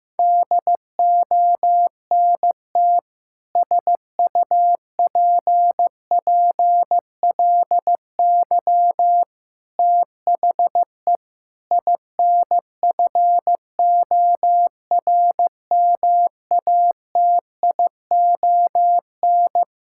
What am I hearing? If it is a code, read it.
Morse code, heard as a keyed tone, about 16 wpm, 700 Hz. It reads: DONT SUPPLY THE INFORMATION
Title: Sentences from Top 1000 Words - Review Part 5 - 15wpm